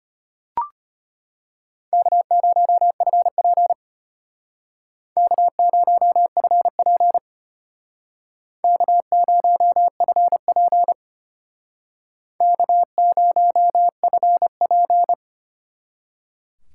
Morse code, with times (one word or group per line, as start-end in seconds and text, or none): 1.9-3.7 K0FP
5.2-7.2 K0FP
8.6-10.9 K0FP
12.4-15.1 K0FP